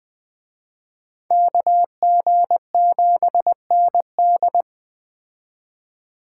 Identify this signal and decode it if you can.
Morse code, heard as a keyed tone, about 20 wpm, 700 Hz. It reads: KG7ND
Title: Speed Racing - USA Call Signs 17wpm